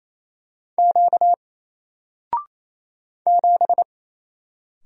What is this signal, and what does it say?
Morse code, heard as a keyed tone, about 27 wpm, 700 Hz. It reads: Q 7
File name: Single-Letter-Number-28wpm